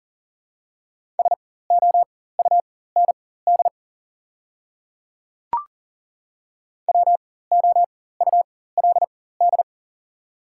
Morse code, heard as a keyed tone, about 40 wpm, spaced as for 19 wpm, 700 Hz. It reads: SOUND WOULD
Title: 4x Character Spacing - Top 100 Words - 40wpm